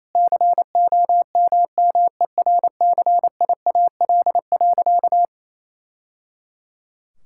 Morse code, heard as a keyed tone, about 28 wpm, 700 Hz. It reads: COMMERCIAL.
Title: Sentences from Top 1000 Words - Review Part 5 - 28wpm